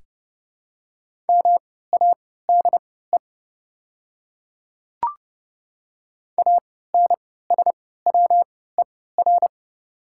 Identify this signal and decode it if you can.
Morse code, heard as a keyed tone, about 30 wpm, spaced as for 17 wpm, 700 Hz. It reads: MADE ANSWER
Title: 3x Character Spacing - Top 300 Words - 30wpm